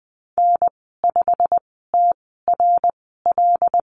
Morse code, heard as a keyed tone, about 19 wpm, spaced as for 14 wpm, 700 Hz. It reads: N5TRL
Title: 088 Lesson 15g - Number 5 - Call Signs 20wpm